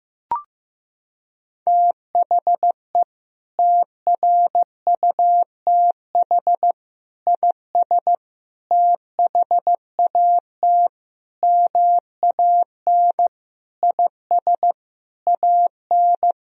Morse code, heard as a keyed tone, about 16 wpm, 700 Hz. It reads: THE TRUTH IS THAT MAN IS AN
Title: Sentences from Top 1000 Words - Review Part 1 - 15wpm